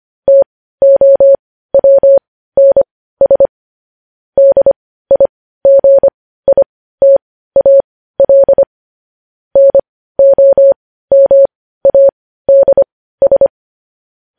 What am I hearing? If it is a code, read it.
Morse code, heard as a keyed tone, about 25 wpm, spaced as for 15 wpm, 550 Hz. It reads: TOWNS DIGITAL NOMADS
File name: podcast / News headlines-1x-25x15